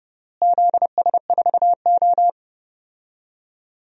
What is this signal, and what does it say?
Morse code, heard as a keyed tone, about 29 wpm, 700 Hz. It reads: ZS4O
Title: DXCC-100 Call Signs - 4 Characters - Repeated 3x - 30wpm